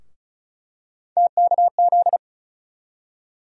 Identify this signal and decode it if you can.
Morse code, heard as a keyed tone, about 36 wpm, 700 Hz. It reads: TKZ
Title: USA Call Sign Postfixes - 3 Letters 35wpm